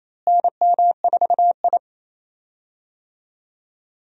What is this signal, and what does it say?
Morse code, heard as a keyed tone, about 28 wpm, 700 Hz. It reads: NM4I
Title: USA Call Signs - 4 Characters - Repeated 3x - 28wpm